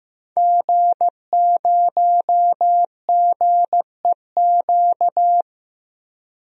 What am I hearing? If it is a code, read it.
Morse code, heard as a keyed tone, about 15 wpm, 700 Hz. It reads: G0GEQ
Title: DXCC-100 Call Signs - 5 Characters - 15wpm